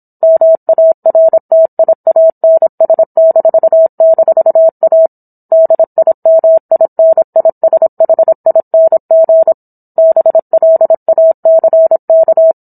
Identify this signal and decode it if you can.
Morse code, heard as a keyed tone, about 27 wpm, 650 Hz. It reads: MARTIANS--A DIMINISHING BLACK